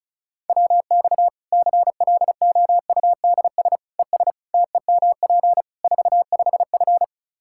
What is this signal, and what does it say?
Morse code, heard as a keyed tone, about 36 wpm, 700 Hz. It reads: WX CLOUDS ES TEMP 45F
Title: Practice QSO - Exchange 2 - Info - 35wpm